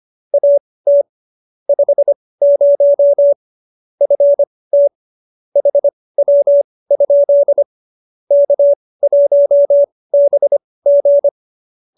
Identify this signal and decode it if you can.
Morse code, heard as a keyed tone, about 25 wpm, spaced as for 18 wpm, 570 Hz. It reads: AT 50 FT HW? K1BG
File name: qso201b 18